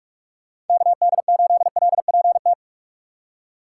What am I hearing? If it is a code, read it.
Morse code, heard as a keyed tone, about 44 wpm, 700 Hz. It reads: KD8LPT
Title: USA Call Signs - 6 Characters - 3x Character Spacing - 45wpm